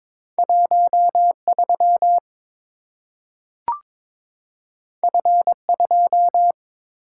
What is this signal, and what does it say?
Morse code, heard as a keyed tone, about 22 wpm, 700 Hz. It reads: 13 F2
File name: practice / 22wpm / Two Letter-Number Combinations - 22wpm